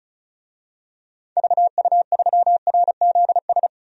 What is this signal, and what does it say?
Morse code, heard as a keyed tone, about 36 wpm, 700 Hz. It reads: VU3RZS
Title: DXCC-100 Call Signs - 6 Characters - 35wpm